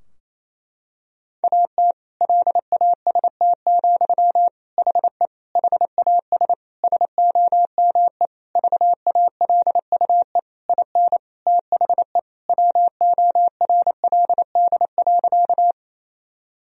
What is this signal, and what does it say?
Morse code, heard as a keyed tone, about 28 wpm, 700 Hz. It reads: AT LAST, HE HAS SOME VALUE IN THE WORLD.
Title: Sentences from Top 1000 Words - Review Part 3 - 28wpm